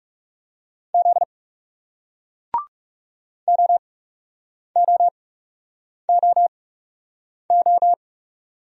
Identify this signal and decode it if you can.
Morse code, heard as a keyed tone, about 41 wpm, spaced as for 8 wpm, 700 Hz. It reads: Z OOOO